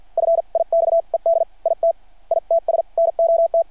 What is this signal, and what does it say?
Morse code, heard as a keyed tone, about 51 wpm, spaced as for 43 wpm, 660 Hz. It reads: FIXED IT ITS NOT
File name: archived / CRSnet